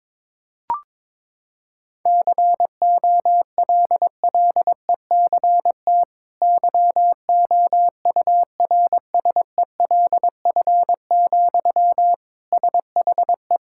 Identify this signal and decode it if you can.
Morse code, heard as a keyed tone, about 21 wpm, 700 Hz. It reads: COLLECT YOURSELF, SHE
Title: Sentences from Top 1600 Words 22wpm